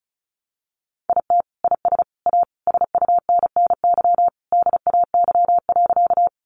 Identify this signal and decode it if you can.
Morse code, heard as a keyed tone, about 34 wpm, 700 Hz. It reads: IT IS A SUNNY DAY.
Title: Additional Easy Phrases 35wpm